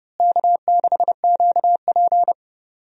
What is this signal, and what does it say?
Morse code, heard as a keyed tone, about 30 wpm, 700 Hz. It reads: K6QP
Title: USA Call Signs - 4 Characters - Repeated 2x - 30wpm